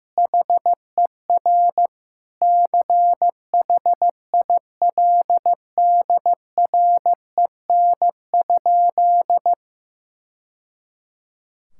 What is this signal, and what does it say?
Morse code, heard as a keyed tone, about 15 wpm, 700 Hz. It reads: HER CHILDREN?
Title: Sentences from Top 1000 Words - Review Part 5 - 15wpm